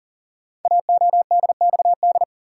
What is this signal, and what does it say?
Morse code, heard as a keyed tone, about 39 wpm, 700 Hz. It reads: AODXD
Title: Code Groups - 5 Letters - 40wpm